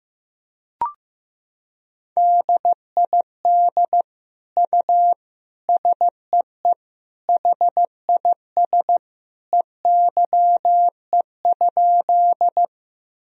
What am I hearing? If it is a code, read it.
Morse code, heard as a keyed tone, about 15 wpm, 700 Hz. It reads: DID U SEE HIS EYE?
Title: Sentences from Top 300 Words and Abbreviations 15wpm